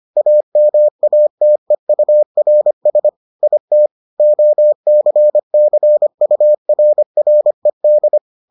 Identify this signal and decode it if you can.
Morse code, heard as a keyed tone, about 25 wpm, 600 Hz. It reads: AMATEURS IT OCCURRED